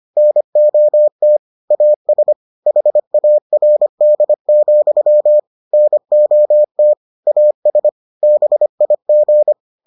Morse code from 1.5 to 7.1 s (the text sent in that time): AS HARD, NOT